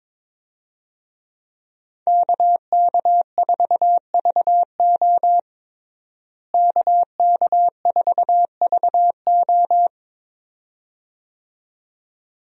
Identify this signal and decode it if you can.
Morse code, heard as a keyed tone, about 22 wpm, 700 Hz. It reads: KK4VO KK4VO
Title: USA Call Signs - 5 Characters - Repeated 2x - 22wpm